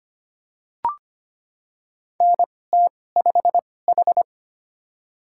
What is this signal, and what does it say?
Morse code, heard as a keyed tone, about 25 wpm, spaced as for 18 wpm, 700 Hz. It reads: NT5H
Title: USA Call Signs - 2x Character Spacing - 25wpm